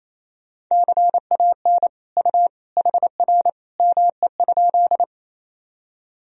Morse code, heard as a keyed tone, about 28 wpm, 700 Hz. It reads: CAN U HR ME?
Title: Sentences from Top 300 Words and Abbreviations 28wpm